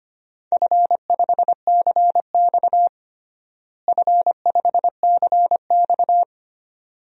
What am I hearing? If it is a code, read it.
Morse code, heard as a keyed tone, about 25 wpm, 700 Hz. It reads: F5CX F5CX
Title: DXCC-100 Call Signs - 4 Characters - Repeated 2x - 25wpm